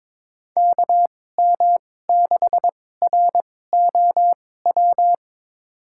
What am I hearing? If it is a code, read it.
Morse code, heard as a keyed tone, about 22 wpm, spaced as for 16 wpm, 700 Hz. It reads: KM6ROW
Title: USA Call Signs - 6 Characters - 2x Character Spacing - 22wpm